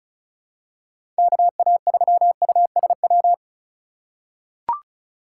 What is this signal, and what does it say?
Morse code, heard as a keyed tone, about 34 wpm, 700 Hz. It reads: KA3USW
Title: USA Call Signs - 6 Characters - 3x Character Spacing - 35wpm